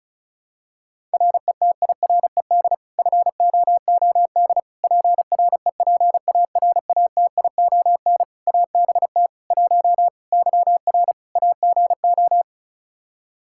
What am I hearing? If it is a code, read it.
Morse code, heard as a keyed tone, about 36 wpm, 700 Hz. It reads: RETIRED FOOD PREPARATION ABT 1 YR AGO
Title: Practice QSO - Exchange 3 - More Info - 35wpm